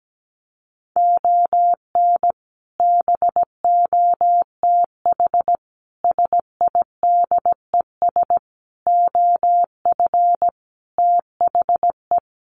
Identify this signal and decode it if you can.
Morse code, heard as a keyed tone, about 17 wpm, 700 Hz. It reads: ON BOTH SIDES OF THE
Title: Sets of 5 Words 17wpm